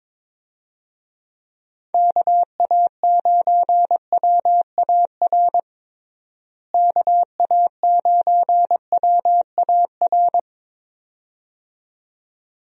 Morse code, heard as a keyed tone, about 21 wpm, 700 Hz. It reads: KA9WAR KA9WAR
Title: USA Call Signs - 6 Characters - Repeated 2x - 22wpm